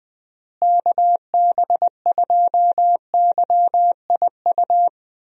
Morse code, heard as a keyed tone, about 20 wpm, 700 Hz. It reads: KB2YIU
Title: USA Call Signs - 6 Characters - Repeated 3x - 20wpm